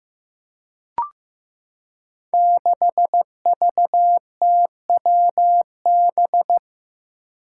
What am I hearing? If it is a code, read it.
Morse code, heard as a keyed tone, about 15 wpm, 700 Hz. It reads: 6VTWB
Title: Five Challenging Letter-Number Combinations 15wpm